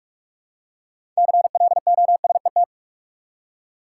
Morse code, heard as a keyed tone, about 43 wpm, 700 Hz. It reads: CLOSET
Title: Six Letter Words - 45wpm